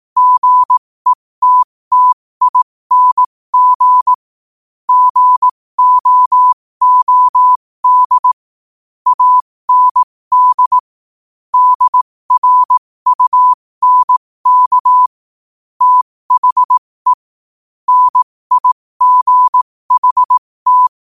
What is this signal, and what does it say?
Morse code, heard as a keyed tone, about 18 wpm, spaced as for 15 wpm, 1000 Hz. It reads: GETTING GOOD AND DRUNK THE NIGHT
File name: qotdfiles / 15wpm